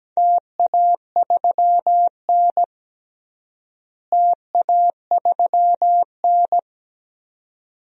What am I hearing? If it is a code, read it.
Morse code, heard as a keyed tone, about 17 wpm, 700 Hz. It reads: TA3N TA3N
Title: DXCC-100 Call Signs - 4 Characters - Repeated 2x - 17wpm